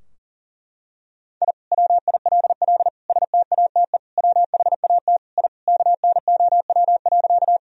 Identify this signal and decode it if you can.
Morse code, heard as a keyed tone, about 37 wpm, 700 Hz. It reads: I WILL STATE WHAT I KNOW.
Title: Sentences from Top 1000 Words - Review Part 2 - 40wpm